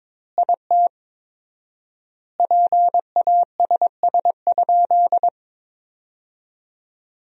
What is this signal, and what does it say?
Morse code, heard as a keyed tone, about 22 wpm, 700 Hz. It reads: IT PASS?
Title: 4x Word-Spacing - Sentences from Top 400 Words - 22wpm